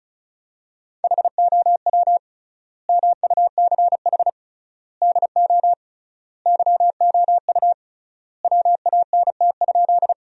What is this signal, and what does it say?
Morse code, heard as a keyed tone, about 35 wpm, 700 Hz. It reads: HOW MUCH DO YOU WANT?
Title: 3x Word-Spacing - Sentences from Top 200 Words - 35wpm